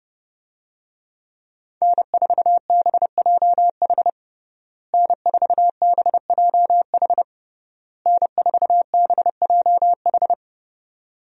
Morse code, heard as a keyed tone, about 30 wpm, 700 Hz. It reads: N4BJH N4BJH N4BJH
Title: USA Call Signs - 5 Characters - Repeated 3x - 30wpm